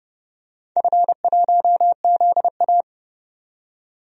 Morse code, heard as a keyed tone, about 30 wpm, 700 Hz. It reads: F1ZA
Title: DXCC-100 Call Signs - 4 Characters - 30wpm